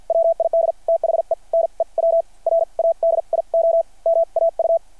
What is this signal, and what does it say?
Morse code, heard as a keyed tone, about 52 wpm, spaced as for 42 wpm, 660 Hz. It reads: WID THE NEW RADIO MAU
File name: archived / CRSnet